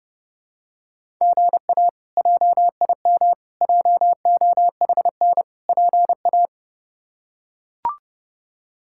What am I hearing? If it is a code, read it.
Morse code, heard as a keyed tone, about 30 wpm, 700 Hz. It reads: GA JIM JOHN PA
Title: K1USN SST QSO Practice - 30wpm